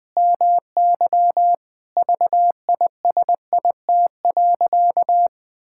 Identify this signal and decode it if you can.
Morse code, heard as a keyed tone, about 20 wpm, 700 Hz. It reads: MY VISIT.
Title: Sentences from Top 1000 Words - Review Part 6 - 20wpm